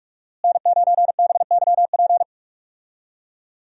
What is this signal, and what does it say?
Morse code, heard as a keyed tone, about 43 wpm, 700 Hz. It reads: N9BYP
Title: USA Call Signs - 5 Characters - Repeated 2x - 45wpm